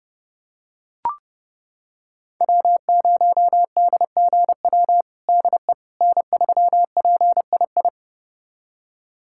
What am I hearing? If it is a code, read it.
Morse code, heard as a keyed tone, about 32 wpm, 700 Hz. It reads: W0DGW DE N3PII
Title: USA Callsign DE Callsign - 5 Characters - 30wpm